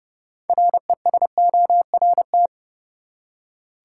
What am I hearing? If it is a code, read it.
Morse code, heard as a keyed tone, about 32 wpm, 700 Hz. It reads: RESORT
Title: Six Letter Words - 30wpm